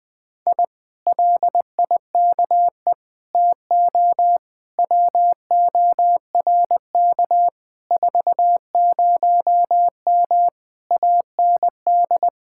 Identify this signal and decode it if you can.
Morse code, heard as a keyed tone, about 20 wpm, 700 Hz. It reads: I LIKE TO WORK 40M AND
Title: Speed-Racing Common QSO Phrases - 20wpm